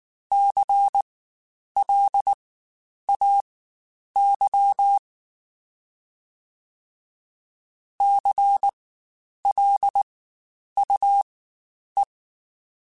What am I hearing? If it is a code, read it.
Morse code, heard as a keyed tone, about 19 wpm, spaced as for 9 wpm, 800 Hz. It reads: CLAY CLUE
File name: Spaced Alphas / Spaced Alphas mp3s / Cs sp @18